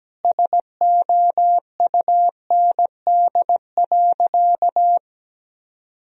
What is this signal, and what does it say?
Morse code, heard as a keyed tone, about 17 wpm, 700 Hz. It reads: SOUND.
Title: Sentences from Top 1000 Words - Review Part 2 - 17wpm